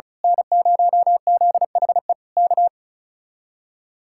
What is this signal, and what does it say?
Morse code, heard as a keyed tone, about 32 wpm, 700 Hz. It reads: N0ZHE K
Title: Speed-Racing Common QSO Phrases Encore - 35wpm